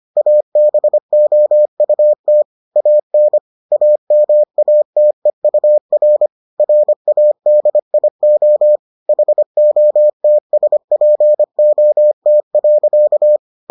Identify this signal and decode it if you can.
Morse code, heard as a keyed tone, about 25 wpm, 600 Hz. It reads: ABOUT AN AMATEUR RADIO HOTSPOT.